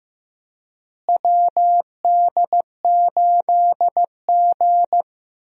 Morse code, heard as a keyed tone, about 16 wpm, 700 Hz. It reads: WD8G
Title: USA Call Signs - 4 Characters - Repeated 2x - 15wpm